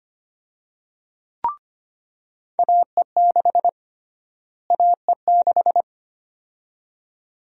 Morse code, heard as a keyed tone, about 25 wpm, 700 Hz. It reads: AE6 AE6
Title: USA Call Sign Prefixes - Repeated 2x - 25wpm